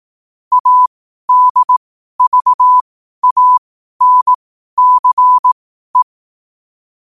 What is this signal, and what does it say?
Morse code, heard as a keyed tone, about 18 wpm, spaced as for 12 wpm, 1000 Hz. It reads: ADVANCE
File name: qotdfiles / 12wpm